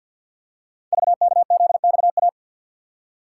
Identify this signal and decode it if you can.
Morse code, heard as a keyed tone, about 48 wpm, 690 Hz. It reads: VKZXA
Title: Code Groups - 5 Letters - 50wpm